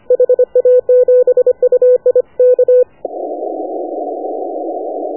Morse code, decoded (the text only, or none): HA7UI K